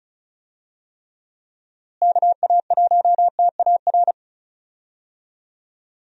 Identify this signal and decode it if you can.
Morse code, heard as a keyed tone, about 34 wpm, 700 Hz. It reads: KA1TAR
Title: USA Call Signs - 6 Characters - Repeated 2x - 35wpm